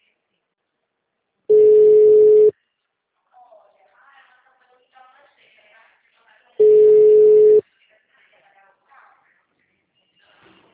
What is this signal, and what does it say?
Ringback tone (Europe, 425 Hz), with speech.
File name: out- -2056- - - .17391